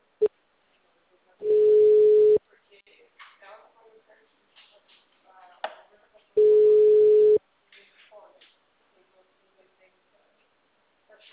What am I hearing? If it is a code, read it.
Ringback tone (Europe, 425 Hz), with speech.